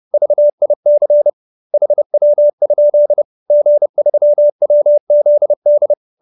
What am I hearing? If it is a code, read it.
Morse code, heard as a keyed tone, about 30 wpm, 590 Hz. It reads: VIC HW? G3WZD
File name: QSO310 30